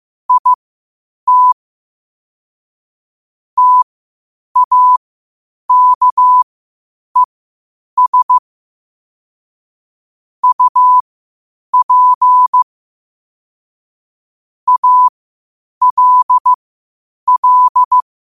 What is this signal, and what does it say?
Morse code, heard as a keyed tone, about 15 wpm, spaced as for 8 wpm, 1000 Hz. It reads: IT TAKES UP ALL
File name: qotdfiles / 8wpm